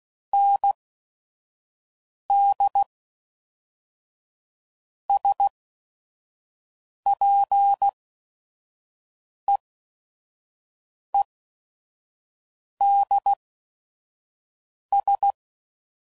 Morse code, heard as a keyed tone, about 16 wpm, spaced as for 5 wpm, 800 Hz. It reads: ND SPEEDS
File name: CW Code Groups @5WPM / Weather Sets / Weather 4 21